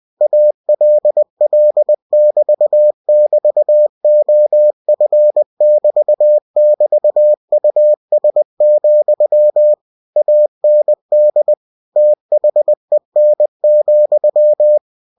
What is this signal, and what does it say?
Morse code, heard as a keyed tone, about 20 wpm, 600 Hz. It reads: ALL<BT><BT>OF<BT><BT>US, AND THEN,